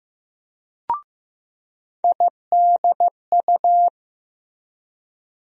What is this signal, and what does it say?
Morse code, heard as a keyed tone, about 15 wpm, 700 Hz. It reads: IDU
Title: USA Call Sign Postfixes - 3 Letters 15wpm